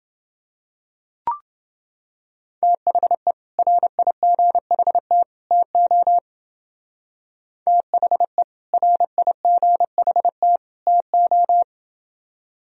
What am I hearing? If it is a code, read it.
Morse code, heard as a keyed tone, about 30 wpm, 700 Hz. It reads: THE RIGHT TO THE RIGHT TO